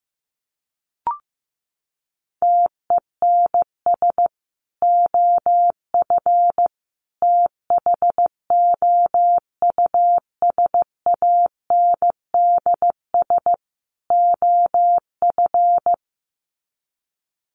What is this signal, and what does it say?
Morse code, heard as a keyed tone, about 15 wpm, 700 Hz. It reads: TENS OF THOUSANDS OF
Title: Sets of 4 Words 15wpm